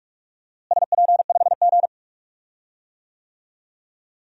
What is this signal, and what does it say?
Morse code, heard as a keyed tone, about 46 wpm, 700 Hz. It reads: SP5G
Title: DXCC-100 Call Signs - 4 Characters - Repeated 3x - 45wpm